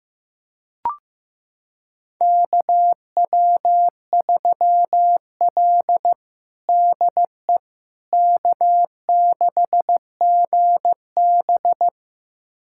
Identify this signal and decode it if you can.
Morse code, heard as a keyed tone, about 15 wpm, 700 Hz. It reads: KW3L DE K6GB
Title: USA Callsign DE Callsign - 4 Characters - 15wpm